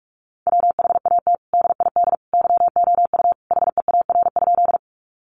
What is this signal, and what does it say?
Morse code, heard as a keyed tone, about 44 wpm, 700 Hz. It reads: WHAT DID YOU HEAR?
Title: Sentences from Top 300 Words 45wpm